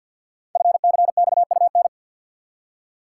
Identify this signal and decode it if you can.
Morse code, heard as a keyed tone, about 50 wpm, 700 Hz. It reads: FCXUN
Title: Code Groups - 5 Letters - 50wpm